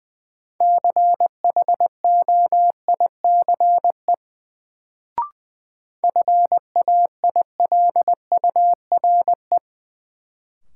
Morse code, heard as a keyed tone, about 20 wpm, 700 Hz. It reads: CHOICE FAILURE
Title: Sentences from Top 900 Words - New Word Drill - 20wpm